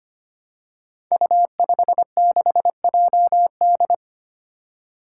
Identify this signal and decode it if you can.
Morse code, heard as a keyed tone, about 25 wpm, 700 Hz. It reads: U56JD
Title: Five Challenging Letter-Number Combinations 25wpm